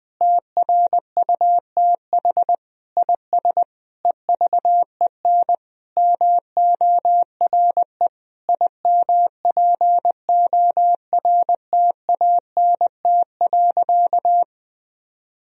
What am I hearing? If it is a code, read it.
Morse code, heard as a keyed tone, about 21 wpm, 700 Hz. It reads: TRUTH IS EVEN MORE IMPORTANT.
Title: Sentences from Top 1000 Words - Review Part 3 - 20wpm